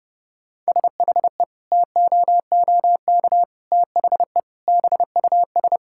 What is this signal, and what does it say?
Morse code, heard as a keyed tone, about 32 wpm, 700 Hz. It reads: SHE TOOK THE BUS
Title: Sentences from Top 1000 Words - Review Part 6 - 30wpm